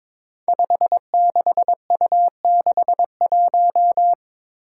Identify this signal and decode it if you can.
Morse code, heard as a keyed tone, about 22 wpm, 700 Hz. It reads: 56U61
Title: Five Challenging Letter-Number Combinations 22wpm